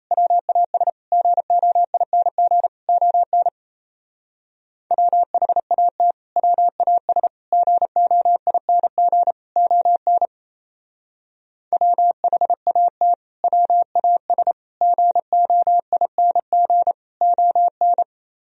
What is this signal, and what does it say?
Morse code, heard as a keyed tone, about 34 wpm, 700 Hz. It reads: WAS GOING ON WHAT WAS GOING ON WHAT WAS GOING ON
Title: Speed-Racing - Sets of 4 Words 28wpm